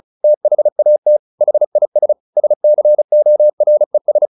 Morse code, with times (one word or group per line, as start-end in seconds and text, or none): 0.2-1.2 THAT
1.4-2.1 HIS
2.4-4.3 SCORES